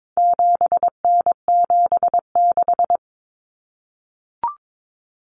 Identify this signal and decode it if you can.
Morse code, heard as a keyed tone, about 22 wpm, 700 Hz. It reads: 7N76
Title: Common Cut Numbers - 4-Digit 22wpm